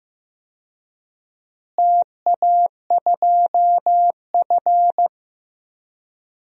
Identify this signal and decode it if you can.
Morse code, heard as a keyed tone, about 15 wpm, 700 Hz. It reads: TA2F
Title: DXCC-100 Call Signs - 4 Characters - Repeated 2x - 15wpm